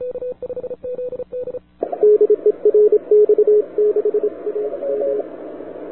Morse code, heard as a keyed tone, about 27 wpm, 420 Hz. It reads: DE RX6AM
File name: wwcw08 / k5zd wwcw08 1130z